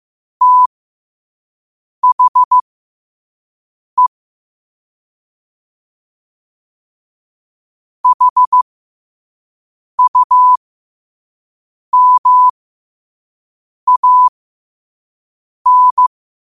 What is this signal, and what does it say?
Morse code, heard as a keyed tone, about 15 wpm, spaced as for 5 wpm, 1000 Hz. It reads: THE HUMAN
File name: qotdfiles / 5wpm